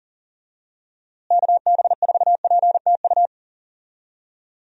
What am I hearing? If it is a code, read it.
Morse code, heard as a keyed tone, about 40 wpm, 700 Hz. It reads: KB4PTU
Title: USA Call Signs - 6 Characters - Repeated 2x - 40wpm